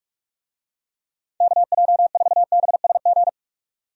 Morse code, heard as a keyed tone, about 46 wpm, 700 Hz. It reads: KJ4BSG
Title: USA Call Signs - 6 Characters - 45wpm